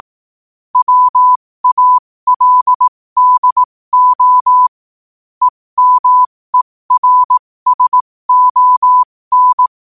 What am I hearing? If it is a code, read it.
Morse code, heard as a keyed tone, about 18 wpm, spaced as for 15 wpm, 1000 Hz. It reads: WALDO EMERSON
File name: qotdfiles / 15wpm